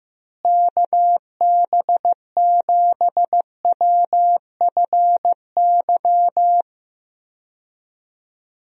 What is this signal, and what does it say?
Morse code, heard as a keyed tone, about 15 wpm, 700 Hz. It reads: KB7WFY
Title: USA Call Signs - 6 Characters - Repeated 2x - 15wpm